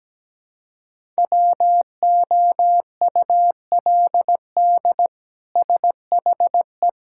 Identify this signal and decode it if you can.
Morse code, heard as a keyed tone, about 17 wpm, 700 Hz. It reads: WOULD SHE
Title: Sentences from Top 1600 Words 17wpm